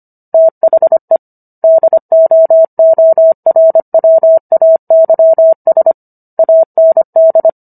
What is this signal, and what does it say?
Morse code, heard as a keyed tone, about 25 wpm, 650 Hz. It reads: THE DOORWAYS AND
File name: Book One / 16-Chapter 16 - 25 WPM